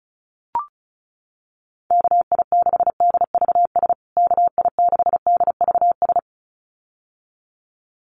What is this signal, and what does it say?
Morse code, heard as a keyed tone, about 34 wpm, 700 Hz. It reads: KI6DVS KI6DVS
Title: USA Call Signs Repeated 2x 35wpm